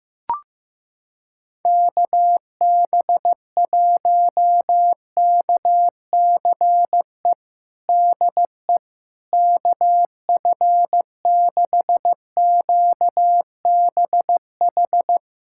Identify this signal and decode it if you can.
Morse code, heard as a keyed tone, about 16 wpm, 700 Hz. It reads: KB1KCE DE KF6QBH
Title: USA Callsign DE Callsign 15wpm